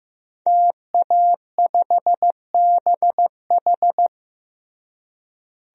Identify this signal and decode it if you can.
Morse code, heard as a keyed tone, about 15 wpm, 700 Hz. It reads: TA5BH
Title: DXCC-100 Call Signs - 5 Characters - Repeated 2x - 15wpm